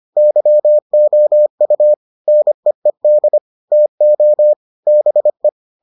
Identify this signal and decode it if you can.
Morse code, heard as a keyed tone, about 25 wpm, 600 Hz. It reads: YOU NEED TO BE